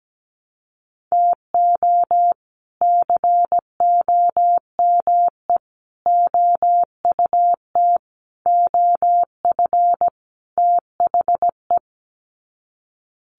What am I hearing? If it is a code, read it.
Morse code, heard as a keyed tone, about 17 wpm, 700 Hz. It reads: TO COME OUT OF THE
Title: Sets of 5 Words Encore2 17wpm